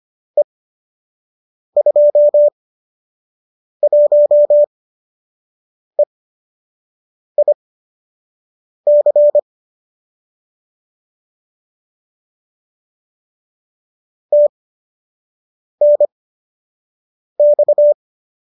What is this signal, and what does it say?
Morse code, heard as a keyed tone, about 25 wpm, spaced as for 6 wpm, 590 Hz. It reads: E21EIC TNX